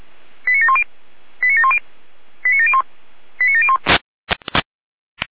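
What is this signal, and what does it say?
Five-tone selective call (ZVEI), heard as quick runs of five tones, tones of about 70 ms. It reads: 89710 89710 89701 89701
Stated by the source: ZVEI five-tone call 89465, received on SDR.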